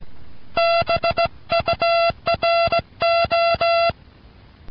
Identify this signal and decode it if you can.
Morse code, heard as a keyed tone, about 17 wpm, 680 Hz, with noise. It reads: BURO